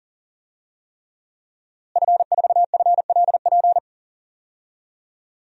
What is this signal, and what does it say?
Morse code, heard as a keyed tone, about 40 wpm, 700 Hz. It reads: F4FLP
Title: DXCC-100 Call Signs - 5 Characters - Repeated 3x - 40wpm